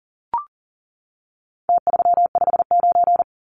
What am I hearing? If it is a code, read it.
Morse code, heard as a keyed tone, about 40 wpm, 700 Hz. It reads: T359
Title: Numbers - Contest Style - 4-Digit 40wpm